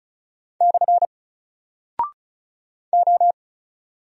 Morse code, heard as a keyed tone, about 36 wpm, 700 Hz. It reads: / O
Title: Single Character (Letter:Number:Punctuation:Prosign) Alt 35wpm